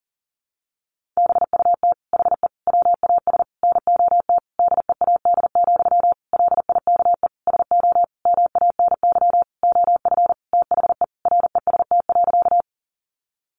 Morse code, heard as a keyed tone, about 40 wpm, 700 Hz. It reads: BUT HE WAS NOT DEAD, LIKE SO MANY OF THE REST.
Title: Sentences from Top 1500 Words 40wpm